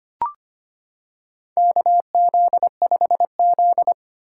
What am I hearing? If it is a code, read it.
Morse code, heard as a keyed tone, about 25 wpm, 700 Hz. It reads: KZ5Z
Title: K1USN SST QSO Practice - 25wpm